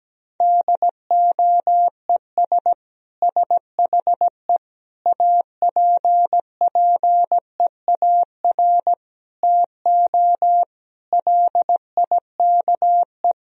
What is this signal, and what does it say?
Morse code, heard as a keyed tone, about 17 wpm, 700 Hz. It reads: DOES SHE APPEAR TO LIKE
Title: Sentences from Top 1000 Words - Review Part 2 - 17wpm